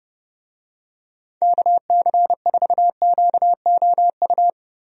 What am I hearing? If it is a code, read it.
Morse code, heard as a keyed tone, about 30 wpm, 700 Hz. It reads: KC4QOU
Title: USA Call Signs - 6 Characters - Repeated 2x - 30wpm